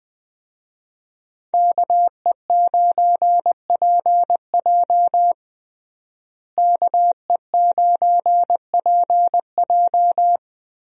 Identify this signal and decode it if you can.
Morse code, heard as a keyed tone, about 21 wpm, 700 Hz. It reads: KE9PJ KE9PJ
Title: USA Call Signs - 5 Characters - Repeated 2x - 20wpm